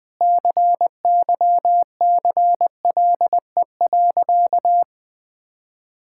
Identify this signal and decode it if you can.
Morse code, heard as a keyed tone, about 20 wpm, 700 Hz. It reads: CYCLE.
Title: Sentences from Top 1000 Words - Review Part 6 - 20wpm